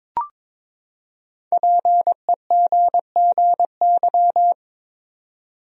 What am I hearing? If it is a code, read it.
Morse code, heard as a keyed tone, about 22 wpm, 700 Hz. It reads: PEGGY
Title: USA Common Names - Max 5 Letters - 22wpm